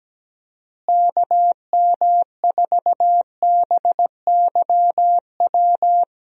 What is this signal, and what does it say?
Morse code, heard as a keyed tone, about 17 wpm, 700 Hz. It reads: KM4BYW
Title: USA Call Signs - 6 Characters - Repeated 3x - 17wpm